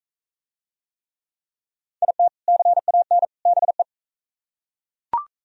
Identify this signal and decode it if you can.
Morse code, heard as a keyed tone, about 39 wpm, 700 Hz. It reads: IT CAN BE